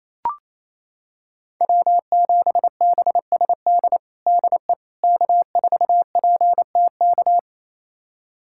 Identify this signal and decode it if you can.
Morse code, heard as a keyed tone, about 27 wpm, 700 Hz. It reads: W7BSD DE K4PTK
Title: USA Callsign DE Callsign - 5 Characters - 28wpm